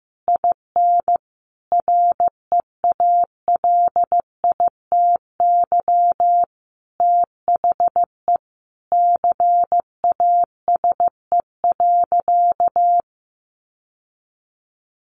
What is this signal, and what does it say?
Morse code, heard as a keyed tone, about 15 wpm, 700 Hz. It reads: IN REALITY THE CASE.
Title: Sentences from Top 1200 Words 15wpm